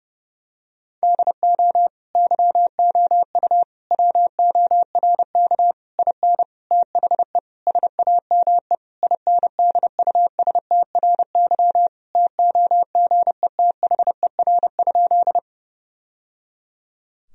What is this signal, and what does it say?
Morse code, heard as a keyed tone, about 30 wpm, 700 Hz. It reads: DO YOU WORK IN THE SAME INDUSTRY TOGETHER?
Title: Sentences from Top 1000 Words - Review Part 4 - 30wpm